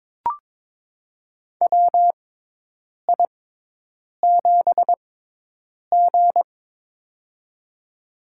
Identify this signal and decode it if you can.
Morse code, heard as a keyed tone, about 22 wpm, spaced as for 8 wpm, 700 Hz. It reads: WI7G
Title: USA Call Signs - 6x Character Spacing - 22wpm